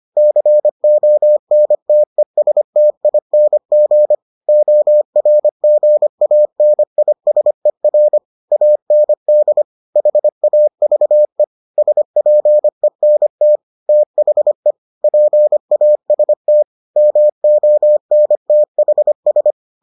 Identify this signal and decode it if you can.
Morse code, heard as a keyed tone, about 25 wpm, 600 Hz. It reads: CONTESTING ORGANISER AND HAVE SPENT THE PAST MONTHS